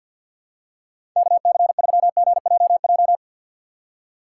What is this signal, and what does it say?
Morse code, heard as a keyed tone, about 50 wpm, 700 Hz. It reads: KC3GJJ
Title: USA Call Signs - 4x Character Spacing - 50wpm